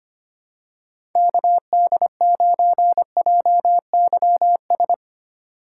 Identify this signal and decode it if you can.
Morse code, heard as a keyed tone, about 25 wpm, 700 Hz. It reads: KD9JYS
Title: USA Call Signs - 6 Characters - 2x Character Spacing - 25wpm